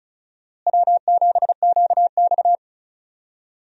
Morse code, heard as a keyed tone, about 36 wpm, 700 Hz. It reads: W7QX
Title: USA Call Signs - 4 Characters - Repeated 2x - 35wpm